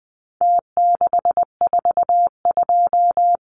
Morse code, heard as a keyed tone, about 20 wpm, 700 Hz. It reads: T642
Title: Numbers - Contest Style - 4-Digit 20wpm